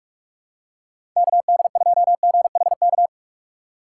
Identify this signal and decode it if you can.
Morse code, heard as a keyed tone, about 43 wpm, 700 Hz. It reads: KD2GHK
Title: USA Call Signs - 6 Characters - 4x Character Spacing - 45wpm